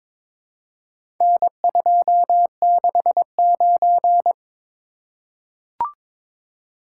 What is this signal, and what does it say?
Morse code, heard as a keyed tone, about 22 wpm, 700 Hz. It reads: N269
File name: practice / 22wpm / Four Letter-Number Combinations - 22wpm